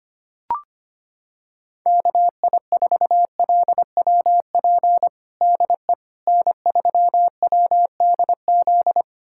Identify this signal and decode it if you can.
Morse code, heard as a keyed tone, about 25 wpm, 700 Hz. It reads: KI4LWP DE N3WDZ
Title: USA Callsign DE Callsign 25wpm